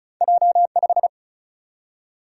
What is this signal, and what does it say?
Morse code, heard as a keyed tone, about 34 wpm, 700 Hz. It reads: J5
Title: Single-Letter Single-Number 35wpm